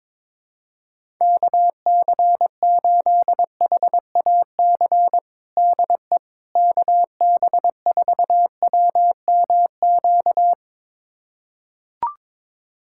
Morse code, heard as a keyed tone, about 22 wpm, 700 Hz. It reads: KC8HAC DE KB4WMQ
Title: USA Callsign DE Callsign - 6 Characters - 22wpm